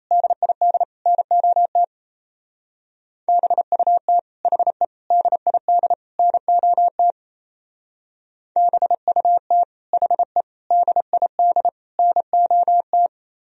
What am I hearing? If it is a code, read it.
Morse code, heard as a keyed tone, about 34 wpm, 700 Hz. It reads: DID NOT BUT HE DID NOT BUT HE DID NOT
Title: Speed-Racing - Sets of 4 Words 28wpm